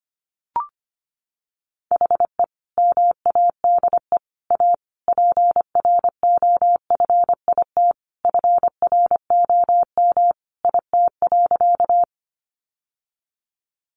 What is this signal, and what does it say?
Morse code, heard as a keyed tone, about 25 wpm, 700 Hz. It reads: HE MADE A PROFIT FROM IT.
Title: Sentences from Top 800 Words 25wpm